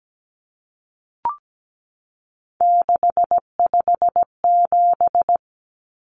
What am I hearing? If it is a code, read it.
Morse code, heard as a keyed tone, about 17 wpm, 700 Hz. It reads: 657
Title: Common Cut Numbers - 3-Digit 17wpm